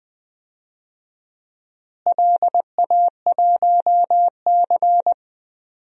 Morse code, heard as a keyed tone, about 20 wpm, 700 Hz. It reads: LA1C
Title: DXCC-100 Call Signs - 4 Characters - Repeated 3x - 20wpm